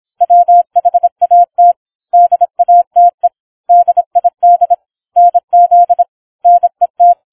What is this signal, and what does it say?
Morse code, heard as a keyed tone, about 26 wpm, 700 Hz. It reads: WHAT DATE DID NZ NET